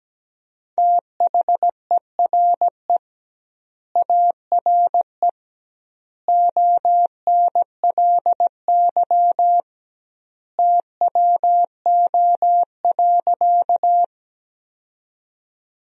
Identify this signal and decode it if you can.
Morse code, heard as a keyed tone, about 17 wpm, 700 Hz. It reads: THERE ARE ONLY TWO.
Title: 2x Word-Spacing - Sentences from Top 200 Words - 17wpm